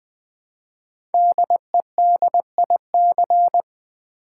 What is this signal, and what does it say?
Morse code, heard as a keyed tone, about 20 wpm, 700 Hz. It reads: DEDIC
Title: Build A Word - 5 Letters At A Time - 10-Letter Words - 20wpm